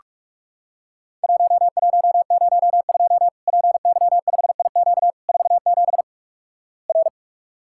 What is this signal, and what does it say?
Morse code, heard as a keyed tone, about 46 wpm, 700 Hz. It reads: 1102 PY5IQ 47 R
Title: Worked All Europe DX Contest - QTC Exchange - 45wpm